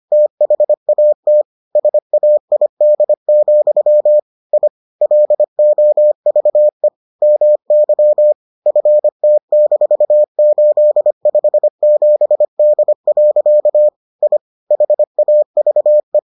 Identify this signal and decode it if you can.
Morse code, heard as a keyed tone, about 25 wpm, 600 Hz. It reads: THAT SAID, I LOVE MY FT-857D. I HAVE